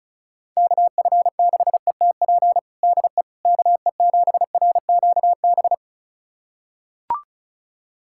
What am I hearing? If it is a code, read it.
Morse code, heard as a keyed tone, about 36 wpm, 700 Hz. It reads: KF6ETP DE KE7RQB
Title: USA Callsign DE Callsign - 6 Characters - 35wpm